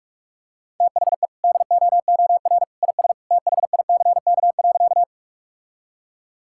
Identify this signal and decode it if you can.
Morse code, heard as a keyed tone, about 48 wpm, 700 Hz. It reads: THE DOOR IS THICK.
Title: Sentences from Top 1600 Words 45wpm